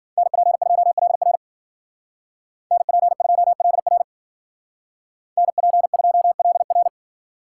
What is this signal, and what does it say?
Morse code, heard as a keyed tone, about 49 wpm, 700 Hz. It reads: NP2LR NP2LR NP2LR
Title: Speed Racing - USA 2x2 Call Signs 40wpm